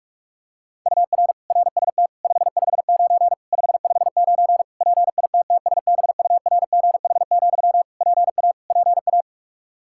Morse code, heard as a keyed tone, about 46 wpm, 700 Hz. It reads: UR RST 559 559 PITTSBURGH, PA PA
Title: Practice QSO - Exchange 1 - Beginning - 45wpm